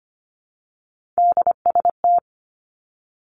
Morse code, heard as a keyed tone, about 25 wpm, 700 Hz. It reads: DST
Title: Three Challenging Letter-Number Combinations 25wpm